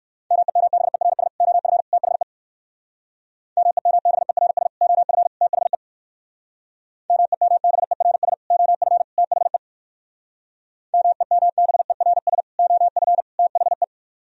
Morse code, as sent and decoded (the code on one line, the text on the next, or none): -- . -- -... .-. ... / --- ..-. / - .... / -- . -- -... .-. ... / --- ..-. / - .... / -- . -- -... .-. ... / --- ..-. / - .... / -- . -- -... .-. ... / --- ..-. / - .... .
MEMBERS OF THE MEMBERS OF THE MEMBERS OF THE MEMBERS OF THE